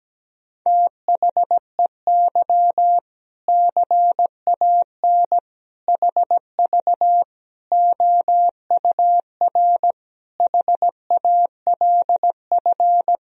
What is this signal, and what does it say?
Morse code, heard as a keyed tone, about 17 wpm, 700 Hz. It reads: THEY CAN HV OUR HALF
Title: Sentences from Top 400 Words and Abbreviations 17wpm